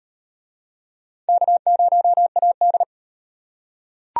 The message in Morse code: K0AD

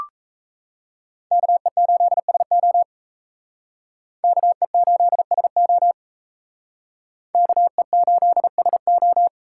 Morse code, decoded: KE8SO KE8SO KE8SO

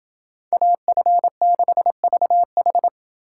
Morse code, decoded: AF6VH

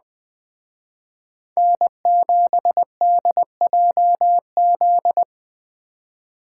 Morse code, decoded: N7DJZ

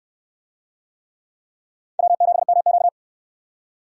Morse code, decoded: W7MQ

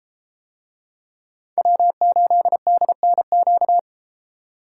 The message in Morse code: W8DNQ

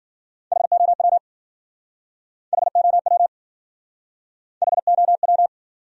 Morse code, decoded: HOW HOW HOW

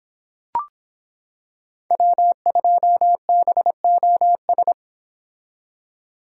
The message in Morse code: W2BOS

